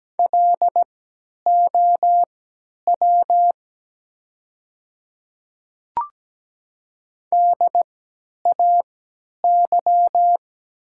Morse code, heard as a keyed tone, about 17 wpm, spaced as for 10 wpm, 700 Hz. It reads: LOW DAY
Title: 3x Character Spacing - Top 200 Words - 17wpm